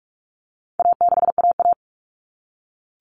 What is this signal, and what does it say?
Morse code, heard as a keyed tone, about 44 wpm, 700 Hz. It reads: A6AA
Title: All Cut Numbers - 4-Digit 45wpm